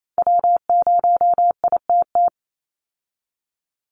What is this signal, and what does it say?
Morse code, heard as a keyed tone, about 28 wpm, 700 Hz. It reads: W0ITT
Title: Call Signs Repeated 3x - 28wpm